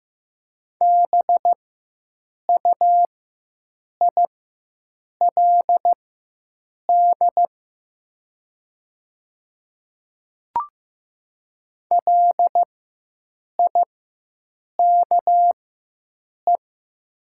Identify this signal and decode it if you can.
Morse code, heard as a keyed tone, about 15 wpm, spaced as for 7 wpm, 700 Hz. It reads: BUILD LIKE
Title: 4x Character Spacing - Top 200 Words - 15wpm